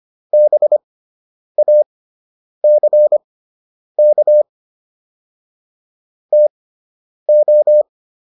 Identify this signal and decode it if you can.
Morse code, heard as a keyed tone, about 25 wpm, spaced as for 9 wpm, 600 Hz. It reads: BACK TO